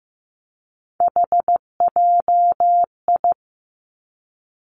Morse code, heard as a keyed tone, about 15 wpm, 700 Hz. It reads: HJI
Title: Three Challenging Letter-Number Combinations 15wpm